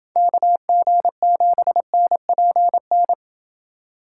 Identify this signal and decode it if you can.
Morse code, heard as a keyed tone, about 27 wpm, 700 Hz. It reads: KG7NPN